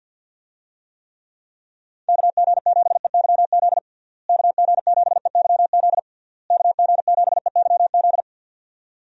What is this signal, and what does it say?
Morse code, heard as a keyed tone, about 50 wpm, 700 Hz. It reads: KG7EYZ KG7EYZ KG7EYZ
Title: USA Call Signs - 6 Characters - Repeated 3x - 50wpm